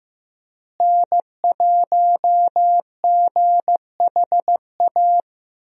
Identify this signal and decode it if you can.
Morse code, heard as a keyed tone, about 15 wpm, 700 Hz. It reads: N1GHA
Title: USA Call Signs - 5 Characters - Repeated 2x - 15wpm